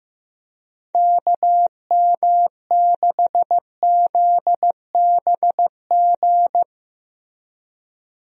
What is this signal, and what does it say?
Morse code, heard as a keyed tone, about 15 wpm, 700 Hz. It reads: KM6ZBG
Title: USA Call Signs - 6 Characters - Repeated 2x - 15wpm